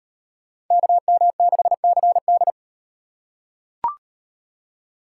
Morse code, heard as a keyed tone, about 36 wpm, 700 Hz. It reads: KM6CD